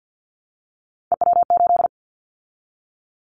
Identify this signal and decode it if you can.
Morse code, heard as a keyed tone, about 50 wpm, 700 Hz. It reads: EW8
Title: Callsign Prefixes for Top 50 DXCC - 50wpm